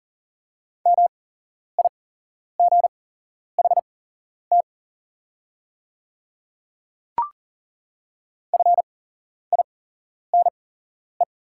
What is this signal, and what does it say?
Morse code, heard as a keyed tone, about 40 wpm, spaced as for 11 wpm, 700 Hz. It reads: MIGHT FINE
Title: 8x Character Spacing - Top 500 Words - 40wpm